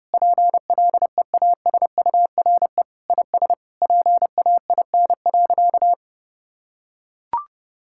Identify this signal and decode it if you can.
Morse code, heard as a keyed tone, about 30 wpm, 700 Hz. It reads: PLEASURE IS PAIN.
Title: Sentences from Top 1600 Words 30wpm